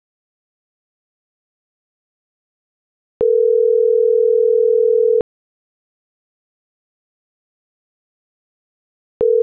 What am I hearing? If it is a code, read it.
Ringback tone (North America), 440 + 480 Hz.